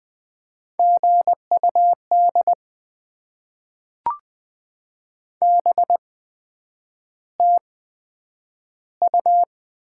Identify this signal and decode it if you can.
Morse code, heard as a keyed tone, about 20 wpm, spaced as for 9 wpm, 700 Hz. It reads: GUD BTU
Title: Top 20 QSO Elements - 8x Character Spacing - 20wpm